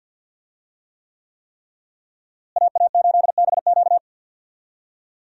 Morse code, heard as a keyed tone, about 51 wpm, 700 Hz. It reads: AA8BQ
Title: USA Call Signs - 5 Characters - Repeated 3x - 50wpm